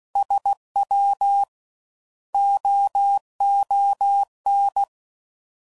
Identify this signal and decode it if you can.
Morse code, heard as a keyed tone, about 16 wpm, spaced as for 12 wpm, 800 Hz. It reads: SWOON